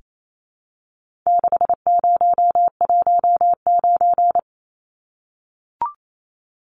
Morse code, heard as a keyed tone, about 28 wpm, 700 Hz. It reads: 6019
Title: Numbers - 4-Digit 28wpm